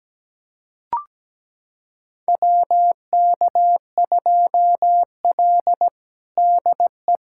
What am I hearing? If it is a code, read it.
Morse code, heard as a keyed tone, about 17 wpm, 700 Hz. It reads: WK2L DE
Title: USA Callsign DE Callsign - 4 Characters - 17wpm